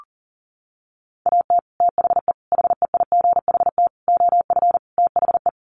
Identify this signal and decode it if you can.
Morse code, heard as a keyed tone, about 40 wpm, 700 Hz. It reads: AT THE HEIGHT OF THE
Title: Sets of 5 Words 40wpm